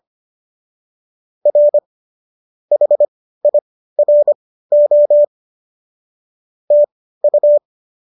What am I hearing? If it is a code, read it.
Morse code, heard as a keyed tone, about 25 wpm, spaced as for 15 wpm, 590 Hz. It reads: R HIRO TU